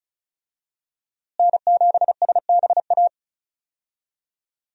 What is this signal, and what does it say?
Morse code, heard as a keyed tone, about 36 wpm, 700 Hz. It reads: N7SBA
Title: USA Call Signs - 5 Characters - Repeated 3x - 35wpm